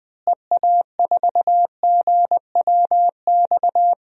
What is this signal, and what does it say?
Morse code, heard as a keyed tone, about 20 wpm, 700 Hz. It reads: EA4GWX
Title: DXCC-100 Call Signs - 6 Characters - Repeated 2x - 20wpm